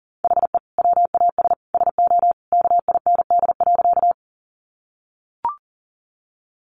Morse code, heard as a keyed tone, about 40 wpm, 700 Hz. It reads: HE WAS SO KIND.
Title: Sentences from Top 200 Words 40wpm